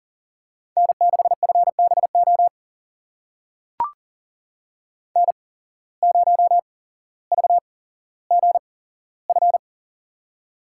Morse code, heard as a keyed tone, about 40 wpm, spaced as for 17 wpm, 700 Hz. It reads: N6FBO N0VGF
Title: USA Call Signs - 5 Characters - 8x Character Spacing - 40wpm